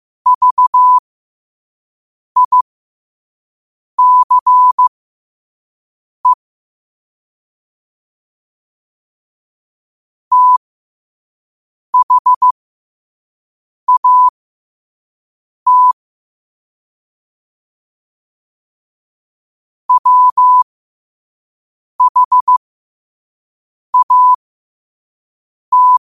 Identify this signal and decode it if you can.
Morse code, heard as a keyed tone, about 15 wpm, spaced as for 5 wpm, 1000 Hz. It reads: VICE THAT WHAT
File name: qotdfiles / 5wpm